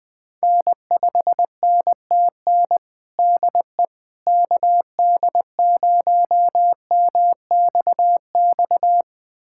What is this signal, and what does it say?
Morse code, heard as a keyed tone, about 20 wpm, 700 Hz. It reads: N5NTN DE KD0MXX
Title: USA Callsign DE Callsign 20wpm